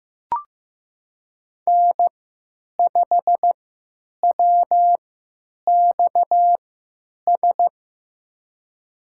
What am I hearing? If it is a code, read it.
Morse code, heard as a keyed tone, about 15 wpm, spaced as for 9 wpm, 700 Hz. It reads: N5WXS
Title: USA Call Signs - 3x Character Spacing - 15wpm